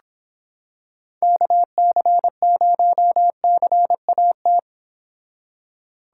Morse code, heard as a keyed tone, about 26 wpm, 700 Hz. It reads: KC0CAT